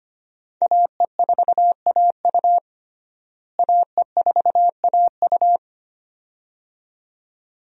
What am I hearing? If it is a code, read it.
Morse code, heard as a keyed tone, about 25 wpm, 700 Hz. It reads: AE4AU AE4AU
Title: USA Call Signs - 5 Characters - Repeated 2x - 25wpm